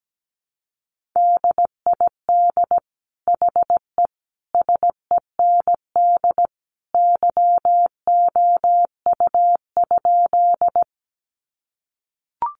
Morse code, heard as a keyed tone, about 17 wpm, 700 Hz. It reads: DID HE SEND YOU?
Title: Sentences from Top 1500 Words 17wpm